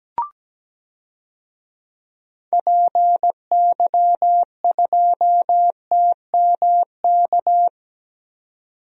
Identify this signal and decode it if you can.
Morse code, heard as a keyed tone, about 17 wpm, 700 Hz. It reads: PY2TMK
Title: DXCC-100 Call Signs - 6 Characters - 17wpm